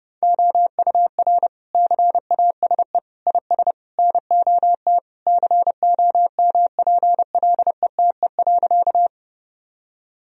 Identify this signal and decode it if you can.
Morse code, heard as a keyed tone, about 30 wpm, 700 Hz. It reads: OUR CASE IS NOT COMPLETE.
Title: Sentences from Top 1000 Words - Review Part 2 - 30wpm